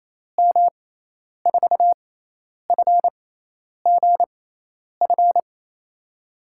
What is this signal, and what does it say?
Morse code, heard as a keyed tone, about 27 wpm, spaced as for 10 wpm, 700 Hz. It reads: M4FGF
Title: USA Call Signs - 6 Characters - 6x Character Spacing - 28wpm